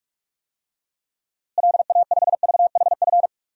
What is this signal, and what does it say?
Morse code, heard as a keyed tone, about 41 wpm, 700 Hz. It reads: LA5VHF
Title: DXCC-100 Call Signs - 6 Characters - Repeated 3x - 45wpm